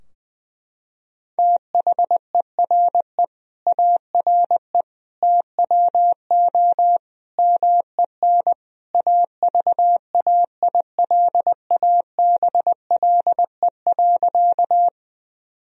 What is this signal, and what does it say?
Morse code, heard as a keyed tone, about 20 wpm, 700 Hz. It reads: THERE ARE TWO MEN AVAILABLE.
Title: Sentences from Top 1000 Words - Review Part 3 - 20wpm